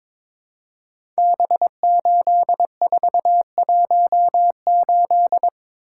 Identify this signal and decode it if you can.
Morse code, heard as a keyed tone, about 22 wpm, 700 Hz. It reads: B8418
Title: Code Groups - 5 Letter:Numbers - 22wpm